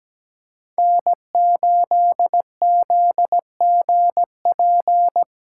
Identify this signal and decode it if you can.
Morse code, heard as a keyed tone, about 17 wpm, 700 Hz. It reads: N8ZGP
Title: USA Callsign DE Callsign - 5 Characters - 17wpm